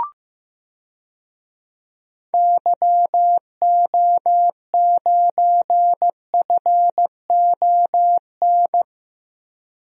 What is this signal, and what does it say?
Morse code, heard as a keyed tone, about 16 wpm, 700 Hz. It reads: YO9FON
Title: DXCC-100 Call Signs - 6 Characters - 15wpm